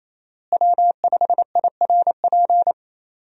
Morse code, heard as a keyed tone, about 28 wpm, 700 Hz. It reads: W5IRP
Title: USA Call Signs - 5 Characters - Repeated 3x - 28wpm